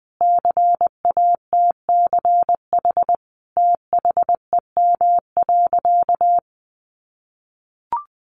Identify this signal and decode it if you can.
Morse code, heard as a keyed tone, about 20 wpm, 700 Hz. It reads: CATCH THEM.
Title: Sentences from Top 1500 Words 20wpm